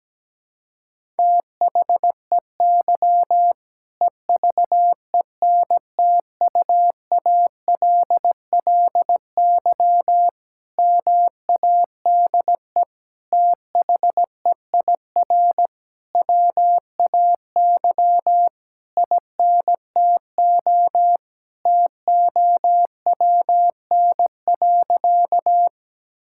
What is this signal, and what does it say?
Morse code, heard as a keyed tone, about 17 wpm, 700 Hz. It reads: THEY EVENTUALLY MADE THEIR WAY INTO TOWN.
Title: Sentences from Top 1000 Words - Review Part 6 - 17wpm